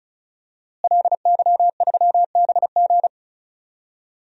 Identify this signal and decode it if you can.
Morse code, heard as a keyed tone, about 36 wpm, 700 Hz. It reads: LY3BG
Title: DXCC-100 Call Signs - 5 Characters - Repeated 2x - 35wpm